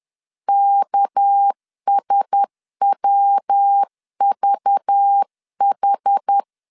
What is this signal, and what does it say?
Morse code, heard as a keyed tone, about 10 wpm, 800 Hz. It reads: KSWVH
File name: CW / DET / 1 6 1500C S C 10wpm 4252